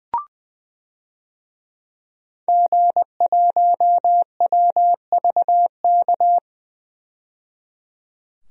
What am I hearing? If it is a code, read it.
Morse code, heard as a keyed tone, about 20 wpm, 700 Hz. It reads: G1WVK
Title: DXCC-100 Call Signs - 5 Characters - 20wpm